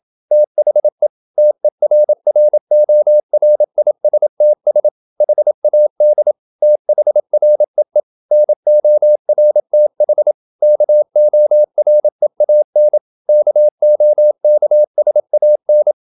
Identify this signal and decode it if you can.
Morse code, heard as a keyed tone, about 27 wpm, 600 Hz. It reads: THE TERRORISTS HAD THREE NORTH KOREAN KOKSAN
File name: Never 0027wpm